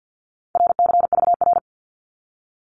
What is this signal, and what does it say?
Morse code, heard as a keyed tone, about 50 wpm, 700 Hz. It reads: RCVR
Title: Abbreviations 50wpm